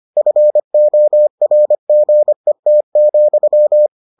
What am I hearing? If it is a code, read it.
Morse code, heard as a keyed tone, about 25 wpm, 600 Hz. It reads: FORGET,